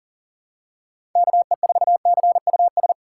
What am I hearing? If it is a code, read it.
Morse code, heard as a keyed tone, about 37 wpm, 700 Hz. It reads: KE4CUS
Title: USA Call Signs - 2x Character Spacing - 40wpm